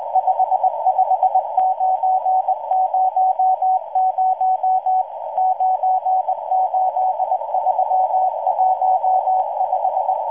Morse code, heard as a keyed tone, about 20 wpm, 750 Hz, with noise. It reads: 55900 98315